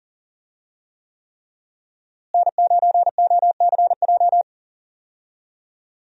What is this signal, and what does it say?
Morse code, heard as a keyed tone, about 40 wpm, 700 Hz. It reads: N9OCJ
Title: USA Call Signs - 5 Characters - Repeated 3x - 40wpm